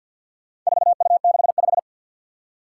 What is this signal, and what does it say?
Morse code, heard as a keyed tone, about 53 wpm, 700 Hz. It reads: 4U65
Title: Four Challenging Letter-Number Combinations 50wpm